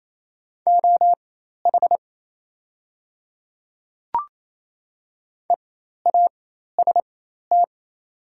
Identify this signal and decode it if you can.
Morse code, heard as a keyed tone, about 28 wpm, spaced as for 13 wpm, 700 Hz. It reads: OH EAST
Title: 4x Character Spacing - Top 500 Words - 28wpm